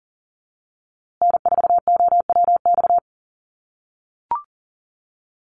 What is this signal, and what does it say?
Morse code, heard as a keyed tone, about 40 wpm, 700 Hz. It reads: N4OWX
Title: USA Call Signs Repeated 3x 40wpm